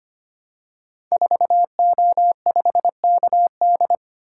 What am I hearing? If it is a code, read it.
Morse code, heard as a keyed tone, about 25 wpm, 700 Hz. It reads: 4O5KD
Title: Code Groups - 5 Letter:Numbers - 25wpm